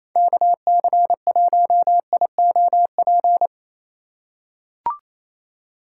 Morse code, heard as a keyed tone, about 28 wpm, 700 Hz. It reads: KC1IOP
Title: USA Call Signs - 6 Characters - 3x Character Spacing - 28wpm